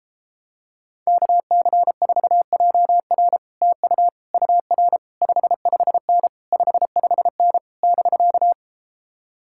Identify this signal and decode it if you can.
Morse code, heard as a keyed tone, about 33 wpm, 700 Hz. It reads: KC4JR TU UR 55N 55N <BK>